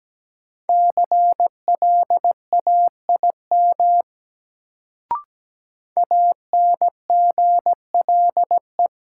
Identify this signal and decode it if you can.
Morse code, heard as a keyed tone, about 17 wpm, 700 Hz. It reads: CLAIM ANGLE
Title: Sentences from Top 1500 Words - New Word Drill Alt - 17wpm